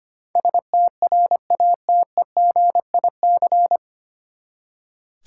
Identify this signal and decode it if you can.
Morse code, heard as a keyed tone, about 25 wpm, 700 Hz. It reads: STRATEGIC
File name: Nine Letter Words - 25wpm